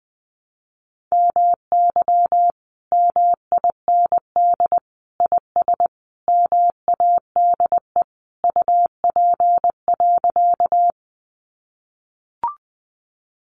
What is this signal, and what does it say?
Morse code, heard as a keyed tone, about 20 wpm, 700 Hz. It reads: MY MIND IS MADE UP.
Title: Sentences from Top 500 Words 20wpm